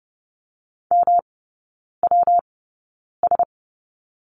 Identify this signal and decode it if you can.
Morse code, heard as a keyed tone, about 30 wpm, spaced as for 9 wpm, 700 Hz. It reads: MWS
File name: Lesson 19e - Letter w - Three Characters 30wpm